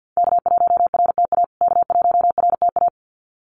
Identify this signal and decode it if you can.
Morse code, heard as a keyed tone, about 48 wpm, 700 Hz. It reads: K1RTA K1RTA
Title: USA Call Signs Repeated 2x 50wpm